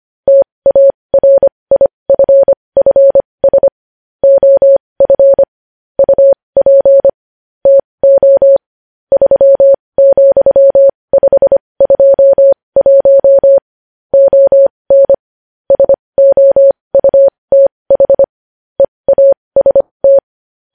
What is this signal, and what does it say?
Morse code, heard as a keyed tone, about 25 wpm, spaced as for 20 wpm, 550 Hz. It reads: TARIFFS OF UP TO 3,521 ON SOUTH EAST
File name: podcast / News headlines-1x-25x20